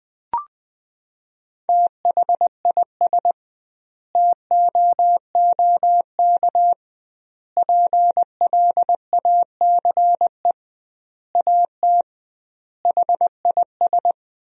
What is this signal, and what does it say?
Morse code, heard as a keyed tone, about 20 wpm, 700 Hz. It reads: THIS TOOK PLACE AT HIS
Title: 2x Word-Spacing - Sentences from Top 300 Words - 20wpm